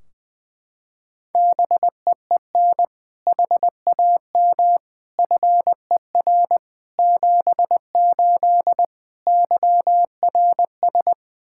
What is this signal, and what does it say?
Morse code, heard as a keyed tone, about 19 wpm, 700 Hz. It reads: BEEN HAM FER 78 YRS
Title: K1USN SST QSO Practice 20wpm